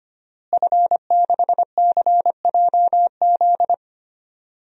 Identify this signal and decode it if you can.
Morse code, heard as a keyed tone, about 25 wpm, 700 Hz. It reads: F6CJZ
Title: DXCC-100 Call Signs - 5 Characters - Repeated 2x - 25wpm